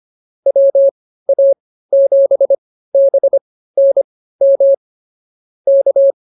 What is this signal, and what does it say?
Morse code, heard as a keyed tone, about 24 wpm, spaced as for 15 wpm, 550 Hz. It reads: WA7BNM K